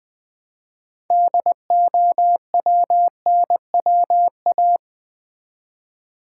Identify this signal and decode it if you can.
Morse code, heard as a keyed tone, about 20 wpm, 700 Hz. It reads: DOWNWA
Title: Build A Word - 3 Letters At A Time - 9-Letter Words - 20wpm